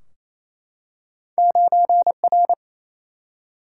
Morse code, heard as a keyed tone, about 28 wpm, 700 Hz. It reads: 9 R
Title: Single-Letter Single-Number 28wpm